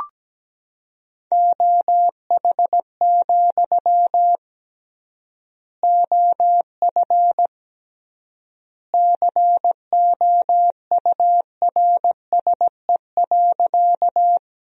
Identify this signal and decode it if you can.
Morse code, heard as a keyed tone, about 17 wpm, 700 Hz. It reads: OH, OF COURSE.
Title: 3x Word-Spacing - Sentences from Top 500 Words - 17wpm